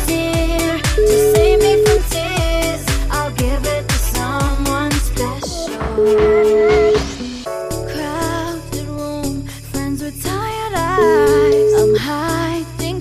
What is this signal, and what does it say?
Ringback tone (Europe, 425 Hz), with noise.